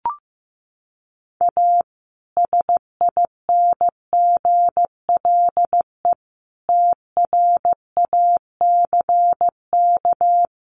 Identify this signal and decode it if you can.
Morse code, heard as a keyed tone, about 15 wpm, 700 Hz. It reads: A SINGLE TRACK
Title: Sentences from Top 1100 Words 15wpm